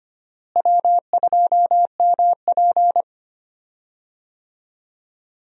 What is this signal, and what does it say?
Morse code, heard as a keyed tone, about 25 wpm, 700 Hz. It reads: W2MP
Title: USA Call Signs - 4 Characters - Repeated 3x - 25wpm